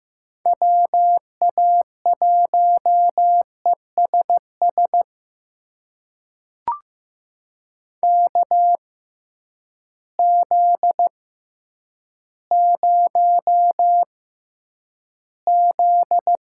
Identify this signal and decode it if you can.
Morse code, heard as a keyed tone, about 15 wpm, spaced as for 9 wpm, 700 Hz. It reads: WA1ESS KZ0Z
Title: USA Call Signs - 6x Character Spacing - 15wpm